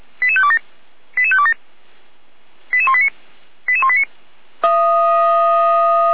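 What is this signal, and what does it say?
Five-tone selective call (ZVEI), heard as quick runs of five tones, tones of about 70 ms. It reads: 88527 88527 88189 88189
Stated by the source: Five-tone ZVEI selective call 88505, received by SDR.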